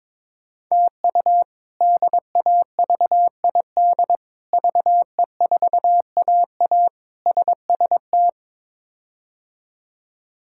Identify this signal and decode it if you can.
Morse code, heard as a keyed tone, about 23 wpm, 700 Hz. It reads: TU DAVID VE4AA SST
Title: K1USN SST QSO Practice - 22wpm